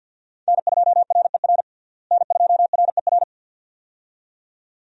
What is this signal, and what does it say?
Morse code, heard as a keyed tone, about 50 wpm, 700 Hz. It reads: N2RER N2RER
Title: USA Call Signs - 5 Characters - Repeated 2x - 50wpm